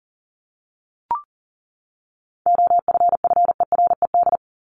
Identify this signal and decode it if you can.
Morse code, heard as a keyed tone, about 40 wpm, 700 Hz. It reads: OFFERED
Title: ED Suffix - Top 100 Words with Base Words 40wpm